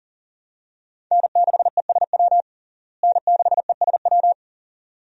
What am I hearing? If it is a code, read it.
Morse code, heard as a keyed tone, about 40 wpm, 700 Hz. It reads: N6ESW N6ESW
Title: USA Call Signs - 5 Characters - Repeated 2x - 40wpm